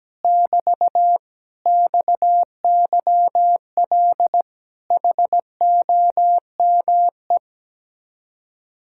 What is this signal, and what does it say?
Morse code, heard as a keyed tone, about 17 wpm, 700 Hz. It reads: <BT> XYL HOME
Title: Speed-Racing Common QSO Phrases - 17wpm